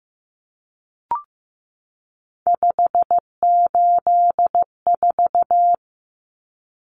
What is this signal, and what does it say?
Morse code, heard as a keyed tone, about 15 wpm, 700 Hz. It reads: 584
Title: Numbers - 3-Digit 15wpm